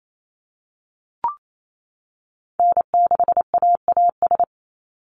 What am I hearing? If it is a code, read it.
Morse code, heard as a keyed tone, about 28 wpm, 700 Hz. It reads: N6AAS
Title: USA Call Signs 28wpm